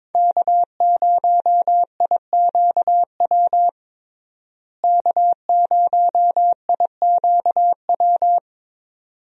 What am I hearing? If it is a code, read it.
Morse code, heard as a keyed tone, about 23 wpm, 700 Hz. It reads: K0IQW K0IQW
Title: USA Call Signs - 5 Characters - Repeated 2x - 22wpm